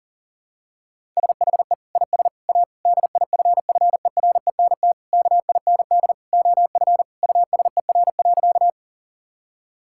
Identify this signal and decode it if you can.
Morse code, heard as a keyed tone, about 40 wpm, 700 Hz. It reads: SHE IS A DIFFERENT KIND OF USER.
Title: Sentences from Top 1000 Words - Review Part 5 - 40wpm